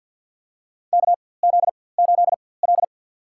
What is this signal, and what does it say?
Morse code, heard as a keyed tone, about 53 wpm, spaced as for 24 wpm, 700 Hz. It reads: KZ8L
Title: USA Call Signs - 4x Character Spacing - 50wpm